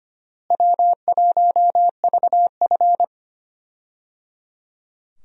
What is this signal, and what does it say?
Morse code, heard as a keyed tone, about 25 wpm, 700 Hz. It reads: W1VF
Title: USA Call Signs - 4 Characters - Repeated 3x - 25wpm